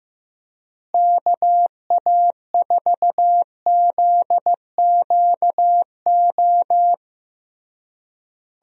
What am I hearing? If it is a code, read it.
Morse code, heard as a keyed tone, about 15 wpm, 700 Hz. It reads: KA4ZQO
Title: USA Call Signs - 6 Characters - Repeated 3x - 15wpm